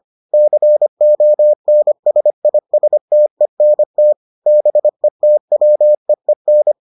Morse code, heard as a keyed tone, about 25 wpm, 600 Hz. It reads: CONSISTENT BETWEEN